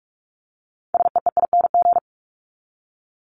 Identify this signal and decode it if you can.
Morse code, heard as a keyed tone, about 44 wpm, 700 Hz. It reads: SEEING
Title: ING Suffix - Top 100 Words with Base Words 45wpm